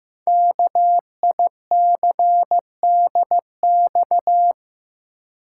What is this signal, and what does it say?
Morse code, heard as a keyed tone, about 15 wpm, 700 Hz. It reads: KICDX
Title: Code Groups - 5 Letter:Numbers - 15wpm